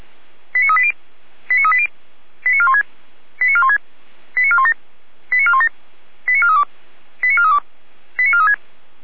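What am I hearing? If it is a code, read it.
Five-tone selective call (ZVEI), heard as quick runs of five tones, tones of about 70 ms. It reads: 89380 89380 89416 89416 89417 89417 89432 89432 89436
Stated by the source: ZVEI five-tone selective call received by SDR.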